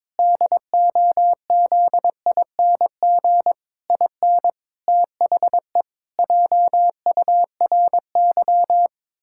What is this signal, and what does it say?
Morse code, heard as a keyed tone, about 22 wpm, 700 Hz. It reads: DOZING IN THE JURY
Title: Pangram Challenge v2 - 22wpm